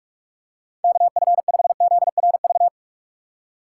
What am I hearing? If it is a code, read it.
Morse code, heard as a keyed tone, about 47 wpm, 700 Hz. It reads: KF5ZRV
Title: USA Call Signs - 6 Characters - 3x Character Spacing - 45wpm